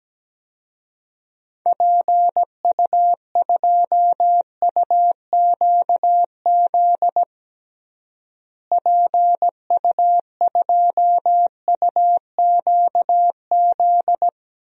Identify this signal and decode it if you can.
Morse code, heard as a keyed tone, about 17 wpm, 700 Hz. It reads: PU2UQZ PU2UQZ
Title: DXCC-100 Call Signs - 6 Characters - Repeated 3x - 17wpm